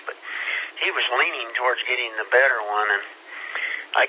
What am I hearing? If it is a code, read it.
Emergency Alert System (SAME) tones, heard as harsh buzzing bursts, with speech. It is heard from 0.2 to 3.8 s.